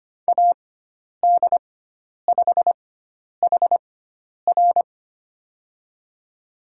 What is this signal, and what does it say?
Morse code, heard as a keyed tone, about 25 wpm, spaced as for 10 wpm, 700 Hz. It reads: AD5HR
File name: Lesson 16g - Letter c - Call Signs 25wpm